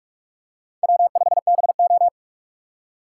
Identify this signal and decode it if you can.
Morse code, heard as a keyed tone, about 44 wpm, 700 Hz. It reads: W5BO